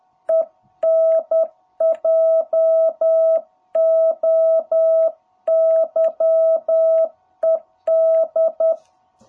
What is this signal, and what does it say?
Morse code, heard as a keyed tone, about 10 wpm, 660 Hz. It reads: ENJOYED
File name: jh8cbh / 2021cw